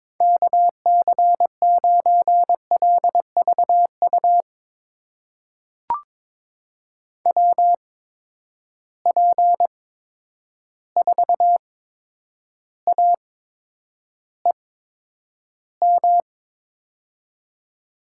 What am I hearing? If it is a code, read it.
Morse code, heard as a keyed tone, about 22 wpm, spaced as for 9 wpm, 700 Hz. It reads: KC9LVU WP4AEM
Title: USA Call Signs - 6 Characters - 8x Character Spacing - 22wpm